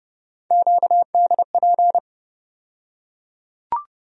Morse code, heard as a keyed tone, about 32 wpm, 700 Hz. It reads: QDP
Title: USA Call Sign Postfixes - 3 Letters 30wpm